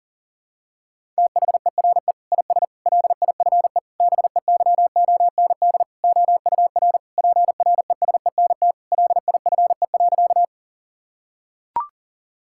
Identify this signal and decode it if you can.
Morse code, heard as a keyed tone, about 37 wpm, 700 Hz. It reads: THERE IS LIFE BEYOND OUR PRESENT LIFE.
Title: Sentences from Top 1000 Words - Review Part 6 - 40wpm